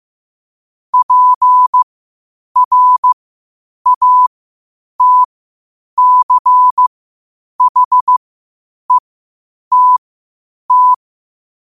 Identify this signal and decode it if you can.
Morse code, heard as a keyed tone, about 15 wpm, spaced as for 8 wpm, 1000 Hz. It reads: PRATCHETT
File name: qotdfiles / 8wpm